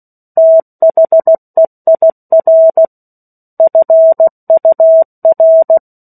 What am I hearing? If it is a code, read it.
Morse code, heard as a keyed tone, about 16 wpm, 650 Hz. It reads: THEIR FUR
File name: 03-Chapter-03 16 WPM